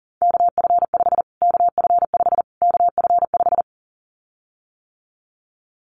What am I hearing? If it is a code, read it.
Morse code, heard as a keyed tone, about 40 wpm, 700 Hz. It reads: KF5 KF5 KF5
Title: Callsign Prefixes for Top 25 DXCC - Repeated 3x - 40wpm